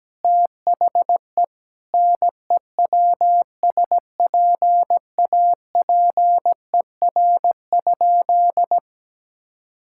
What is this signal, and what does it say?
Morse code, heard as a keyed tone, about 17 wpm, 700 Hz. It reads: THE NEWSPAPER?
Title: Sentences from Top 1000 Words - Review Part 6 - 17wpm